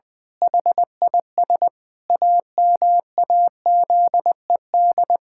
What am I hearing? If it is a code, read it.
Morse code, heard as a keyed tone, about 20 wpm, 700 Hz. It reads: HIS AMAZED
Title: Pangram Challenge v2 - 20wpm